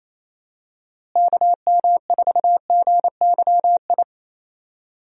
Morse code, heard as a keyed tone, about 28 wpm, 700 Hz. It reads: KM4GYI
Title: USA Call Signs - 6 Characters - Repeated 2x - 28wpm